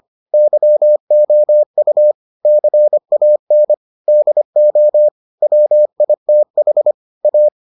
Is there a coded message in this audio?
Yes: Morse code from 0.3 to 7.5 s, heard as a keyed tone, about 25 wpm, 600 Hz. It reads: YOU CAN DO WITH A